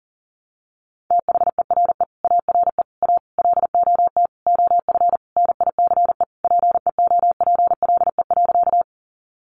Morse code, heard as a keyed tone, about 40 wpm, 700 Hz. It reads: THERE ARE A LOT OF NICE PEOPLE.
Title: Sentences from Top 1100 Words 40wpm